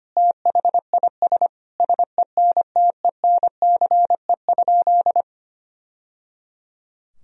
Morse code, heard as a keyed tone, about 25 wpm, 700 Hz. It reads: THIS SENTENCE?
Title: Sentences from Top 1000 Words - Review Part 1 - 25wpm